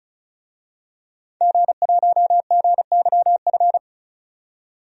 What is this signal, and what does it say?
Morse code, heard as a keyed tone, about 36 wpm, 700 Hz. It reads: G1GYF
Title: DXCC-100 Call Signs - 5 Characters - 35wpm